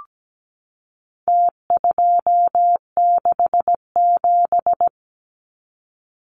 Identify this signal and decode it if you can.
Morse code, heard as a keyed tone, about 17 wpm, 700 Hz. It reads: T267
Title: Numbers - Contest Style - 4-Digit 17wpm